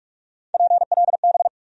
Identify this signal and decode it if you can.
Morse code, heard as a keyed tone, about 45 wpm, 700 Hz. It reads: PLB
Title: USA Call Sign Postfixes - 3 Letters 45wpm